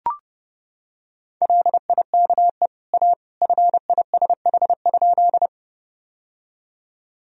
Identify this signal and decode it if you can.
Morse code, heard as a keyed tone, about 32 wpm, 700 Hz. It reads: LIKE A FISH?
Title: Sentences from Top 1000 Words - Review Part 1 - 30wpm